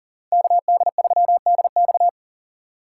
Morse code, heard as a keyed tone, about 48 wpm, 700 Hz. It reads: KD3DX
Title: Speed Racing - USA 2x2 Call Signs 40wpm